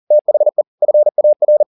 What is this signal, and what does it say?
Morse code, heard as a keyed tone, about 40 wpm, 600 Hz. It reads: THE FAR
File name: the old man and the sea 2 40wpm